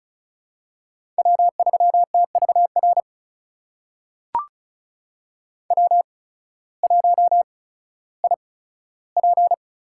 Morse code, heard as a keyed tone, about 36 wpm, spaced as for 16 wpm, 700 Hz. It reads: W3TVR W1IP
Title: USA Call Signs - 8x Character Spacing - 35wpm